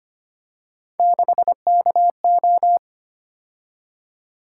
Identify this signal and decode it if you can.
Morse code, heard as a keyed tone, about 25 wpm, 700 Hz. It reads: 6KO
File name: Three Letter-Number Combinations - 25wpm